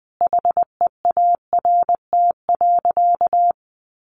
Morse code, heard as a keyed tone, about 20 wpm, 700 Hz. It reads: HEART.
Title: Sentences from Top 1200 Words 20wpm